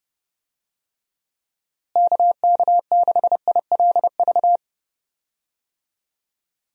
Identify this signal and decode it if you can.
Morse code, heard as a keyed tone, about 32 wpm, 700 Hz. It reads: KK6ILV
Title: USA Call Signs - 6 Characters - Repeated 2x - 30wpm